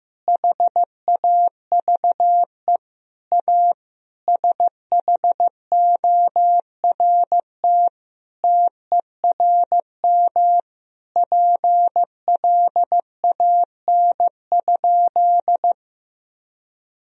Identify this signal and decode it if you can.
Morse code, heard as a keyed tone, about 15 wpm, 700 Hz. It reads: HAVE A SHORT TERM PLAN?
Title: Sentences from Top 1000 Words - Review Part 4 - 15wpm